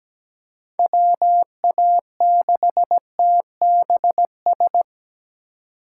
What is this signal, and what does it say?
Morse code, heard as a keyed tone, about 17 wpm, 700 Hz. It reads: WA6TBS
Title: USA Call Signs - 6 Characters - 4x Character Spacing - 17wpm